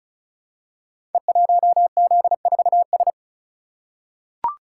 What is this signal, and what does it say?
Morse code, heard as a keyed tone, about 36 wpm, 700 Hz. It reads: E1Z4S
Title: Code Groups - 5 Letter:Numbers - 35wpm